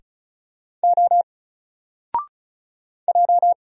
Morse code, heard as a keyed tone, about 36 wpm, 700 Hz. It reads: O J
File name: Single-Letter-Number-35wpm